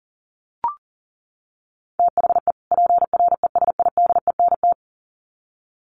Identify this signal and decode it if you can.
Morse code, heard as a keyed tone, about 40 wpm, 700 Hz. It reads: THE PRESIDENT
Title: Sets of 2 Words 40wpm